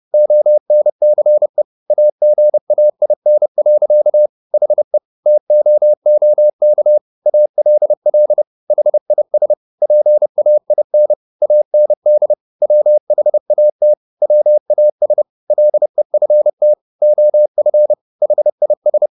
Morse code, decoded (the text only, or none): ONCE AGAIN. HE TOOK ALL HIS PAIN AND WHAT WAS LEFT OF HIS